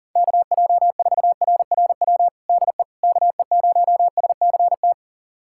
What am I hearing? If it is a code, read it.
Morse code, heard as a keyed tone, about 40 wpm, 700 Hz. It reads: KJ4RRW DE KE0SCT
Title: USA Callsign DE Callsign - 6 Characters - 40wpm